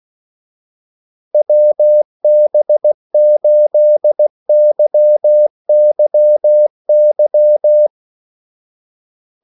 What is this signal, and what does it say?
Morse code, heard as a keyed tone, about 16 wpm, 600 Hz. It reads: WB8YYY
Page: Kaldesignaler 16 wpm